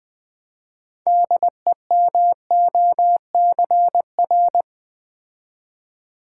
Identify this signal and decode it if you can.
Morse code, heard as a keyed tone, about 20 wpm, 700 Hz. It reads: DEMOCR
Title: Build A Word - 3 Letters At A Time - 9-Letter Words - 20wpm